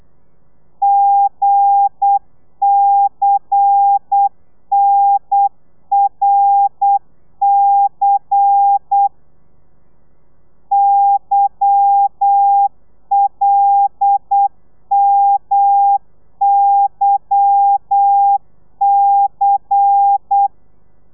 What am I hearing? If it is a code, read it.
Morse code, heard as a keyed tone, about 8 wpm, 800 Hz. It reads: GCNRC YLMYC
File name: telegrafia / i2tfj / Lez10 35